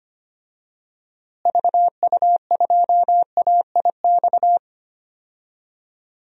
Morse code, heard as a keyed tone, about 25 wpm, 700 Hz. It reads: VU2AIX
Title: DXCC-100 Call Signs - 6 Characters - Repeated 2x - 25wpm